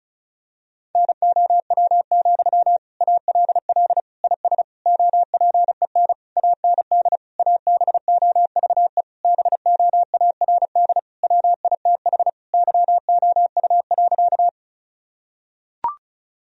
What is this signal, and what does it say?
Morse code, heard as a keyed tone, about 34 wpm, 700 Hz. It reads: NOW, ALL IS OPEN AND ABOVE BOARD WITH YOU.
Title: Sentences from Top 1000 Words - Review Part 4 - 35wpm